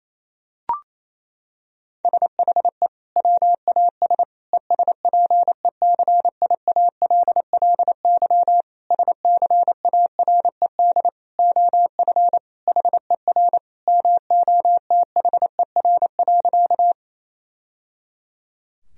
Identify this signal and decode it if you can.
Morse code, heard as a keyed tone, about 28 wpm, 700 Hz. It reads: SHE WAS ESPECIALLY SCARED OF HER MOTHER.
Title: Sentences from Top 1000 Words - Review Part 4 - 28wpm